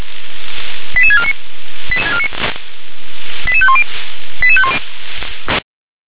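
Five-tone selective call (ZVEI), heard as quick runs of five tones, tones of about 70 ms. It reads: 88649 88510
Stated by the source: Five-tone ZVEI selective call received by SDR.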